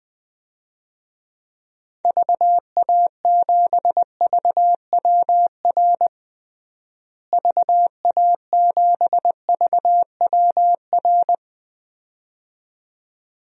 Morse code, heard as a keyed tone, about 20 wpm, 700 Hz. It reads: VA7VWR VA7VWR
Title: DXCC-100 Call Signs - 6 Characters - Repeated 2x - 20wpm